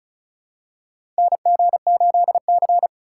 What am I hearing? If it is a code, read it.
Morse code, heard as a keyed tone, about 34 wpm, 700 Hz. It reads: NG8C
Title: USA Call Signs - 3x Character Spacing - 35wpm